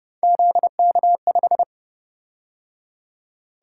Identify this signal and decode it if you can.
Morse code, heard as a keyed tone, about 29 wpm, 700 Hz. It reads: ZK5
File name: Speed-Racing - Three Letter-Number Combinations - 25wpm